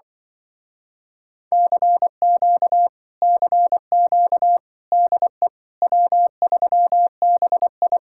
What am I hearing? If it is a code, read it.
Morse code, heard as a keyed tone, about 24 wpm, 700 Hz. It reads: CQ CQ DE W3BI
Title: Speed-Racing Common QSO Phrases - 20wpm